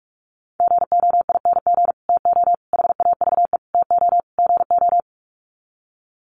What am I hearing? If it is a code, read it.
Morse code, heard as a keyed tone, about 44 wpm, 700 Hz. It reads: GOING TO HAVE TO GO
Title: Sets of 5 Words Encore 45wpm